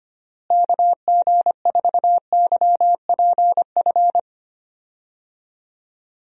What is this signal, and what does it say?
Morse code, heard as a keyed tone, about 25 wpm, 700 Hz. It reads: KG4YPF
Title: USA Call Signs - 6 Characters - Repeated 3x - 25wpm